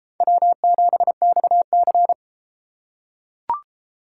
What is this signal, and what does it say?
Morse code, heard as a keyed tone, about 33 wpm, 700 Hz. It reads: W7XC